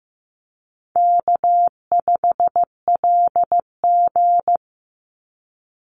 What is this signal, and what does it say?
Morse code, heard as a keyed tone, about 15 wpm, 700 Hz. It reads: K5LG
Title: Easy Call Signs Repeated 2x - 15wpm